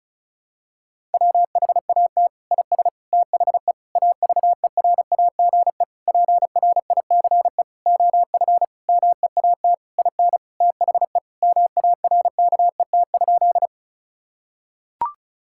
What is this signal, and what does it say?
Morse code, heard as a keyed tone, about 36 wpm, 700 Hz. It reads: WHAT IS THE AVERAGE PRICE OF MEAT IN THE MARKET?
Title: Sentences from Top 1000 Words - Review Part 6 - 35wpm